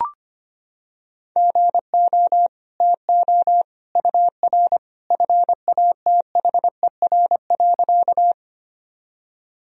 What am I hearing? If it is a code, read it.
Morse code, heard as a keyed tone, about 25 wpm, 700 Hz. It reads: GO TO UR FATHER.
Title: Sentences from Top 300 Words and Abbreviations 25wpm